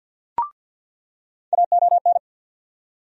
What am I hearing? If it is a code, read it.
Morse code, heard as a keyed tone, about 46 wpm, 700 Hz. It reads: AON
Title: USA Call Sign Postfixes - 3 Letters 50wpm